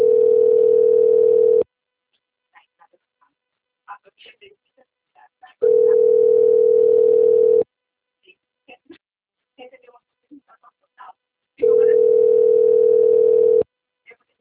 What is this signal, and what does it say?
Ringback tone (North America), 440 + 480 Hz, with speech.
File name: rg-1600-unknown- - - .17730